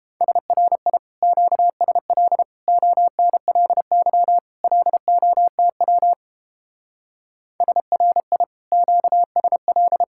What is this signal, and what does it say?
Morse code, heard as a keyed tone, about 31 wpm, 700 Hz. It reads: SRI QSL ONLY LOTW SRI QSL